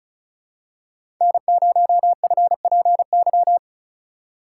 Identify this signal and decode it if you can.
Morse code, heard as a keyed tone, about 36 wpm, 700 Hz. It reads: N0FPY
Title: USA Call Signs - 5 Characters - 3x Character Spacing - 35wpm